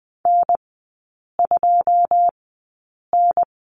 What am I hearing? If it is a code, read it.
Morse code, heard as a keyed tone, about 20 wpm, spaced as for 8 wpm, 700 Hz. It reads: N2N
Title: 107 Lesson 18e - Letter m - Three Characters 20wpm